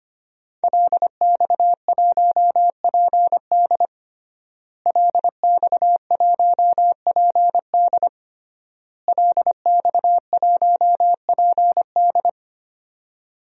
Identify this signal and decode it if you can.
Morse code, heard as a keyed tone, about 25 wpm, 700 Hz. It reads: LX1PD LX1PD LX1PD
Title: DXCC-100 Call Signs - 5 Characters - Repeated 3x - 25wpm